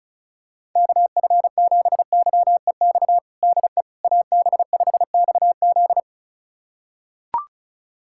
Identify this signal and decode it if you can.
Morse code, heard as a keyed tone, about 34 wpm, 700 Hz. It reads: KF7YEX DE AB5XZ
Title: USA Callsign DE Callsign 35wpm